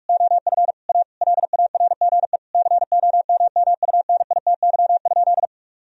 Morse code, heard as a keyed tone, about 41 wpm, 700 Hz. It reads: OF A LARGE COMMUNITY?
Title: Sentences from Top 1000 Words - Review Part 4 - 45wpm